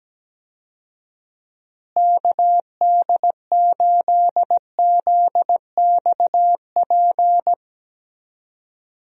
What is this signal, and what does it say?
Morse code, heard as a keyed tone, about 17 wpm, 700 Hz. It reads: KD8ZXP
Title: USA Call Signs - 6 Characters - Repeated 3x - 17wpm